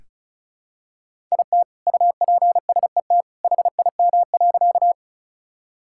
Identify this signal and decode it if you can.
Morse code, heard as a keyed tone, about 36 wpm, 700 Hz. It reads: IT UPSET HIM.
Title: Sentences from Top 1000 Words - Review Part 5 - 35wpm